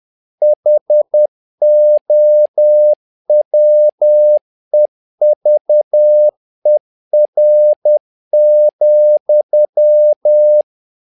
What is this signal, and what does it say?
Morse code, heard as a keyed tone, about 10 wpm, 600 Hz. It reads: HOWEVER,
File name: learning center / tower trouble / tower trouble 10wpm